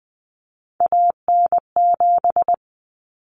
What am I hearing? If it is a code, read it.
Morse code, heard as a keyed tone, about 20 wpm, 700 Hz. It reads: AN7
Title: Common Cut Numbers - 3-Digit 20wpm